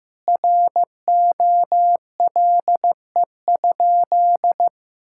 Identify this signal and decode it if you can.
Morse code, heard as a keyed tone, about 15 wpm, 700 Hz. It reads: ROLE?
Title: Sentences from Top 1000 Words - Review Part 6 - 15wpm